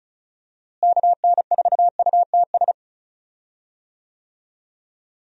Morse code, heard as a keyed tone, about 36 wpm, 700 Hz. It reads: KN4UTS
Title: USA Call Signs - 6 Characters - Repeated 3x - 35wpm